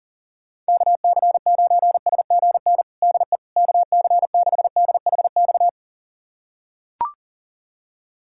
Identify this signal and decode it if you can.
Morse code, heard as a keyed tone, about 40 wpm, 700 Hz. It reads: KC9SGN DE KC6DHX
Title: USA Callsign DE Callsign 40wpm